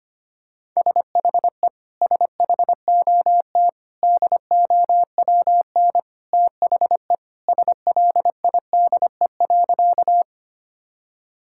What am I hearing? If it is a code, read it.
Morse code, heard as a keyed tone, about 25 wpm, 700 Hz. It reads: SHE SHOT DOWN THE SLIDE.
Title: Sentences from Top 1600 Words 25wpm